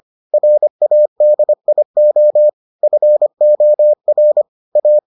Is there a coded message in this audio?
Yes: Morse code from 0.3 to 5.0 s, heard as a keyed tone, about 25 wpm, 600 Hz. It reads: RADIO FOR A